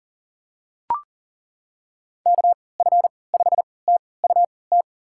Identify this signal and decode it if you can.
Morse code, heard as a keyed tone, about 39 wpm, spaced as for 23 wpm, 700 Hz. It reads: KF5TUT
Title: USA Call Signs - 6 Characters - 3x Character Spacing - 40wpm